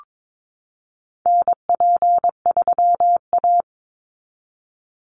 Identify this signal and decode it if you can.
Morse code, heard as a keyed tone, about 22 wpm, 700 Hz. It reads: NP3A
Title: Call Signs Repeated 3x - 22wpm